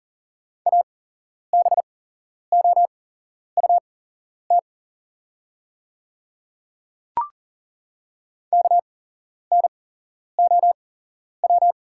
Morse code, heard as a keyed tone, about 40 wpm, spaced as for 11 wpm, 700 Hz. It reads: ABOUT KNOW
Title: 8x Character Spacing - Top 100 Words - 40wpm